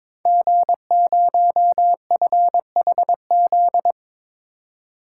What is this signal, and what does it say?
Morse code, heard as a keyed tone, about 23 wpm, 700 Hz. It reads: G0FHZ
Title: Code Groups - 5 Letter:Numbers - 22wpm